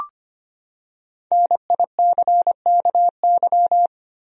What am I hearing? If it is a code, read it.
Morse code, heard as a keyed tone, about 25 wpm, 700 Hz. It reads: NICKY
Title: USA Common Names - Max 5 Letters - 25wpm